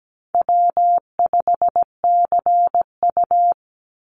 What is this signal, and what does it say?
Morse code, heard as a keyed tone, about 17 wpm, 700 Hz. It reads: W5CU
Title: Call Signs Repeated 3x - 17wpm